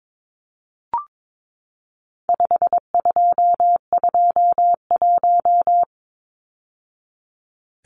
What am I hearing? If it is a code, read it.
Morse code, heard as a keyed tone, about 22 wpm, 700 Hz. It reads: 5221
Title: Numbers - 4-Digit 22wpm